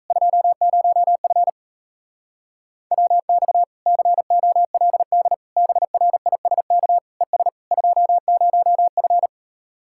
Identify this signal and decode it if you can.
Morse code, heard as a keyed tone, about 38 wpm, 700 Hz. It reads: 20F WX COLD BRISK ES 20F